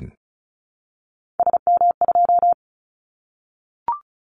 Morse code, heard as a keyed tone, about 34 wpm, 700 Hz. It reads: SM2
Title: Callsign Prefixes for Top 25 DXCC - Repeated 3x - 35wpm